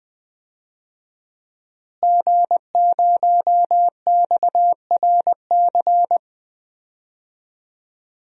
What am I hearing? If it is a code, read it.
Morse code, heard as a keyed tone, about 20 wpm, 700 Hz. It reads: G0XRC
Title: DXCC-100 Call Signs - 5 Characters - Repeated 2x - 20wpm